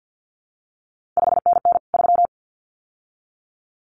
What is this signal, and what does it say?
Morse code, heard as a keyed tone, about 50 wpm, 700 Hz. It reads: 5NN 3
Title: CQ WW DX Contest Exchange - 50wpm